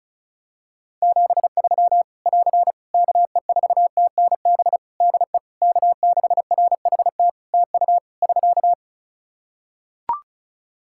Morse code, heard as a keyed tone, about 34 wpm, 700 Hz. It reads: 73 <AR> KE4TNB DE K6RHT TU <SK>
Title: Practice QSO - Exchange 4 - Ending - 35wpm